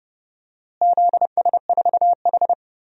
Morse code, heard as a keyed tone, about 32 wpm, 700 Hz. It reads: ZS4H
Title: DXCC-100 Call Signs - 4 Characters - Repeated 3x - 30wpm